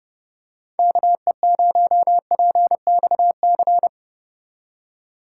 Morse code, heard as a keyed tone, about 30 wpm, 700 Hz. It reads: KE0PXC